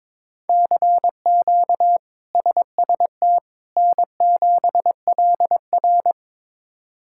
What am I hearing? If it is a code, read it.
Morse code, heard as a keyed tone, about 22 wpm, 700 Hz. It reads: CQ SST N7LR
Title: K1USN SST QSO Practice - 22wpm